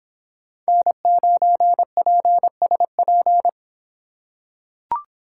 Morse code, heard as a keyed tone, about 25 wpm, 700 Hz. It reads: N9PSP